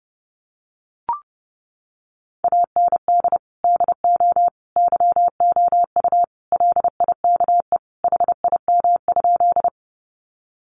Morse code, heard as a keyed tone, about 30 wpm, 700 Hz. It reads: AND DO YOU LIKE HIM?
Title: Sentences from Top 100 Words 30wpm